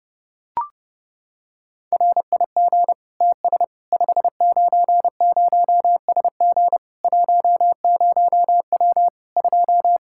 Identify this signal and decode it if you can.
Morse code, heard as a keyed tone, about 30 wpm, 700 Hz. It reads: RIG TS 590SG 10W 2
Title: Practice QSO - Exchange 2 - Info - 30wpm